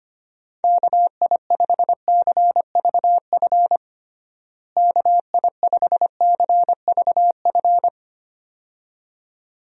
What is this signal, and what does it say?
Morse code, heard as a keyed tone, about 25 wpm, 700 Hz. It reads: KI5CVF KI5CVF
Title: USA Call Signs - 6 Characters - Repeated 2x - 25wpm